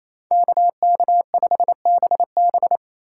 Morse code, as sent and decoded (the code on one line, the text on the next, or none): -.- -.- ..... -... -...
KK5BB